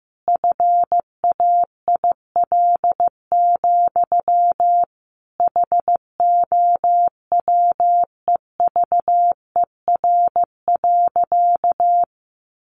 Morse code, heard as a keyed tone, about 14 wpm, 700 Hz. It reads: FAIL, HOWEVER.
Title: Sentences from Top 1100 Words 15wpm